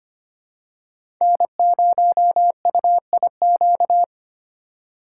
Morse code, heard as a keyed tone, about 25 wpm, 700 Hz. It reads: N0UIQ
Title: USA Call Signs - 5 Characters - 4x Character Spacing - 25wpm